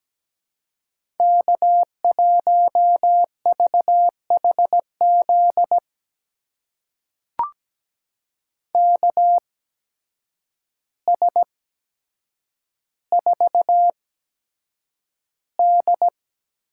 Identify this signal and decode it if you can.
Morse code, heard as a keyed tone, about 17 wpm, spaced as for 8 wpm, 700 Hz. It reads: K1VHZ KS4D
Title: USA Call Signs - 5 Characters - 8x Character Spacing - 17wpm